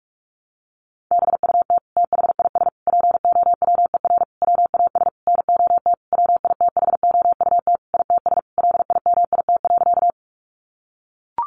Morse code, heard as a keyed tone, about 44 wpm, 700 Hz. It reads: BUT THIS POWER WAS NOT WITHOUT ITS LIMIT.
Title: Sentences from Top 1500 Words 45wpm